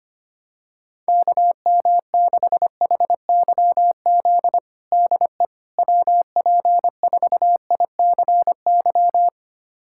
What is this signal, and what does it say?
Morse code, heard as a keyed tone, about 25 wpm, 700 Hz. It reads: KM6HYZ DE WP4ICY
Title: USA Callsign DE Callsign - 6 Characters - 25wpm